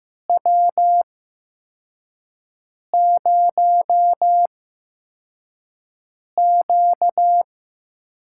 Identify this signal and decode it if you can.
Morse code, heard as a keyed tone, about 15 wpm, spaced as for 4 wpm, 700 Hz. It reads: W0Q
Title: USA Call Signs - 8x Character Spacing - 15wpm